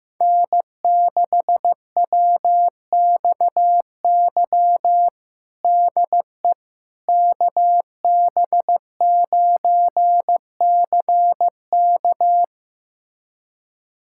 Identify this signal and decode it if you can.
Morse code, heard as a keyed tone, about 15 wpm, 700 Hz. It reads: N6WXY DE KB9CK
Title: USA Callsign DE Callsign - 5 Characters - 15wpm